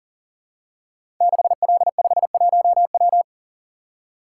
Morse code, heard as a keyed tone, about 41 wpm, 700 Hz. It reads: 6L51W
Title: Code Groups - 5 Letter:Numbers - 40wpm